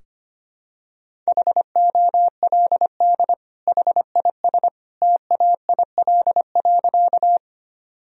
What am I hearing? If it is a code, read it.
Morse code, heard as a keyed tone, about 25 wpm, 700 Hz. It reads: HOLD HIS TAIL.
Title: Sentences from Top 1000 Words - Review Part 3 - 25wpm